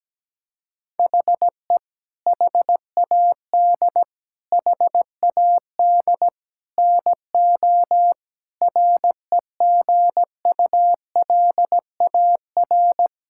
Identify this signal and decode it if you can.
Morse code, heard as a keyed tone, about 17 wpm, 700 Hz. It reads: HE HAD HAD NO REGULAR
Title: Sentences from Top 1000 Words - Review Part 6 - 17wpm